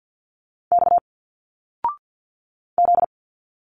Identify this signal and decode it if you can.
Morse code, heard as a keyed tone, about 51 wpm, 700 Hz. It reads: X Z
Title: Single Letters 50wpm